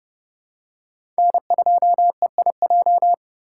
Morse code, heard as a keyed tone, about 30 wpm, 700 Hz. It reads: N2EIJ
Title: USA Call Signs - 3x Character Spacing - 30wpm